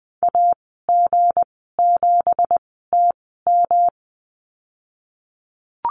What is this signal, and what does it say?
Morse code, heard as a keyed tone, about 19 wpm, spaced as for 14 wpm, 700 Hz. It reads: AG7TM
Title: 173 Lesson 28g - Number 9 - Call Signs 20wpm